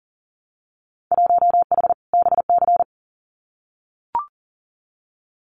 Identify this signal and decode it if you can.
Morse code, heard as a keyed tone, about 42 wpm, 700 Hz. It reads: 1H BC
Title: Winter Field Day Exchanges Encore 40wpm